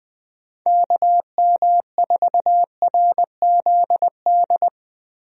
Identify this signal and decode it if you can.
Morse code, heard as a keyed tone, about 20 wpm, 700 Hz. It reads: KM4RZD
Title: USA Call Signs - 6 Characters - Repeated 2x - 20wpm